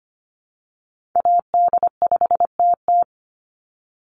Morse code, heard as a keyed tone, about 25 wpm, 700 Hz. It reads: AD5TT
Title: Call Signs Repeated 3x - 25wpm